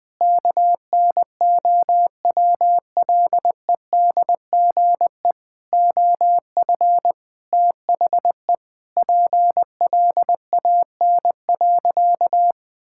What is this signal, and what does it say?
Morse code, heard as a keyed tone, about 20 wpm, 700 Hz. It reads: KNOWLEDGE OF THE PLAN.
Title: Sentences with 8-Word Phrases - 20wpm